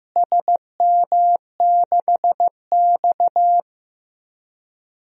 Morse code, heard as a keyed tone, about 15 wpm, 700 Hz. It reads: SM6X
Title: DXCC-100 Call Signs - 4 Characters - Repeated 2x - 15wpm